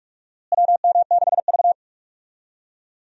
Morse code, heard as a keyed tone, about 46 wpm, 700 Hz. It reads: WM6V
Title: USA Call Signs - 4 Characters - Repeated 2x - 45wpm